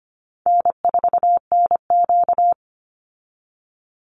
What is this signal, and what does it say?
Morse code, heard as a keyed tone, about 25 wpm, 700 Hz. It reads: N4NQ
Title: Easy Call Signs - 25wpm